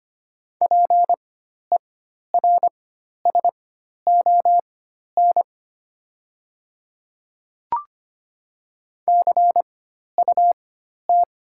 Morse code, heard as a keyed tone, about 25 wpm, spaced as for 12 wpm, 700 Hz. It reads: PERSON CUT
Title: 4x Character Spacing - Top 500 Words - 25wpm